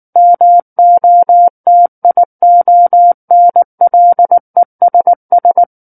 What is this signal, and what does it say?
Morse code, heard as a keyed tone, about 19 wpm, 700 Hz. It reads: MOTIONLESS
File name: Book One / 10-Chapter 10 - 19 WPM